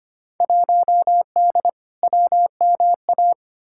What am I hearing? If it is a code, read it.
Morse code, heard as a keyed tone, about 25 wpm, 700 Hz. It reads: 1D WMA
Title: Field Day Exchanges - 25wpm